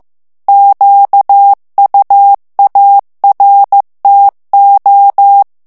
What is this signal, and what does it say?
Morse code, heard as a keyed tone, about 15 wpm, 800 Hz. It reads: QUARTO
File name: morse mp3 / ITALIEN texte clair 2 vitesse 14 mots minute